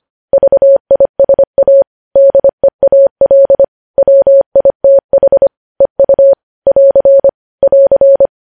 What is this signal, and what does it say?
Morse code, heard as a keyed tone, about 25 wpm, 550 Hz. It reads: VISA DEAL WITH EU <AR> <AR>
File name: podcast / News headlines-1x-25x25